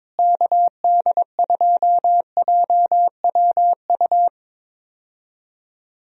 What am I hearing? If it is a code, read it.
Morse code, heard as a keyed tone, about 21 wpm, 700 Hz. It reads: KD2JWU
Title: USA Call Signs - 6 Characters - Repeated 3x - 22wpm